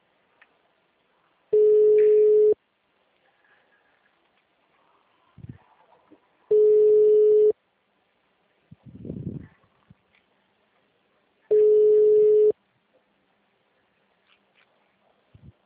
Ringback tone (Europe, 425 Hz).
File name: out- -2001- - - .0